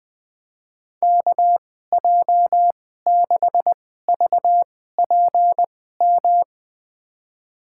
Morse code, heard as a keyed tone, about 21 wpm, spaced as for 15 wpm, 700 Hz. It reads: KJ6VPM
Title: USA Call Signs - 2x Character Spacing - 20wpm